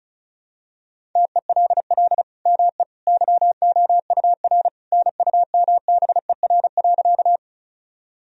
Morse code, heard as a keyed tone, about 36 wpm, 700 Hz. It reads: TELL ME YOUR NUMBER.
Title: Sentences from Top 1000 Words - Review Part 1 - 35wpm